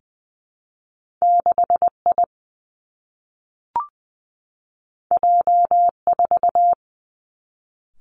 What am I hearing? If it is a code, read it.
Morse code, heard as a keyed tone, about 20 wpm, 700 Hz. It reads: 6 I J 4
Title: Two Challenging Letter-Number Combinations 20wpm